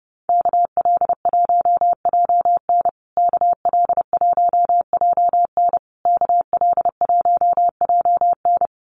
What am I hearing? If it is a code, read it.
Morse code, heard as a keyed tone, about 30 wpm, 700 Hz. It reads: KL1JN KL1JN KL1JN
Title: USA Call Signs Repeated 3x 30wpm